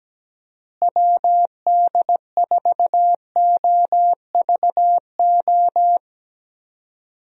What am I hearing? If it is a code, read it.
Morse code, heard as a keyed tone, about 17 wpm, 700 Hz. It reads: WD4OVO
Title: USA Call Signs - 6 Characters - Repeated 2x - 17wpm